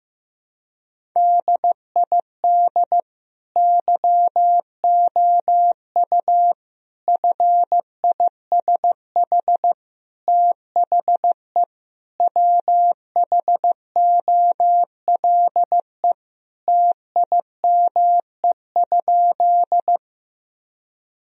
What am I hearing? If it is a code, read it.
Morse code, heard as a keyed tone, about 15 wpm, 700 Hz. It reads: DID YOU FISH THE WHOLE TIME?
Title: Sentences from Top 1000 Words - Review Part 1 - 15wpm